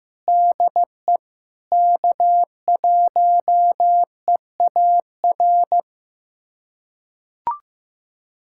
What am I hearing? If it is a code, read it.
Morse code, heard as a keyed tone, about 15 wpm, 700 Hz. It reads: DE K1EAR
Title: USA Callsign DE Callsign 15wpm